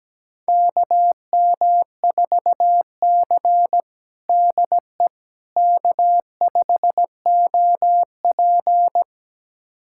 Morse code, heard as a keyed tone, about 17 wpm, 700 Hz. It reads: KM4C DE K5OP
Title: USA Callsign DE Callsign - 4 Characters - 17wpm